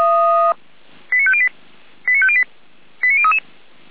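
Five-tone selective call (ZVEI), heard as quick runs of five tones, tones of about 70 ms. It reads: 89508 89508 89033
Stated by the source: ZVEI five-tone selective call received on an SDR, logged by the Feuerwehr Orlamünde fire brigade server.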